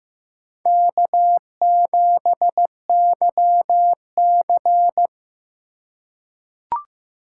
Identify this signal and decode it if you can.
Morse code, heard as a keyed tone, about 15 wpm, 700 Hz. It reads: K7YC
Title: USA Call Signs - 4 Characters - 2x Character Spacing - 15wpm